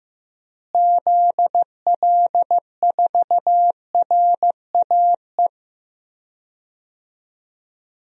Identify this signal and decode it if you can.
Morse code, heard as a keyed tone, about 15 wpm, 700 Hz. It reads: ZL4RAE
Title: DXCC-100 Call Signs - 6 Characters - Repeated 3x - 15wpm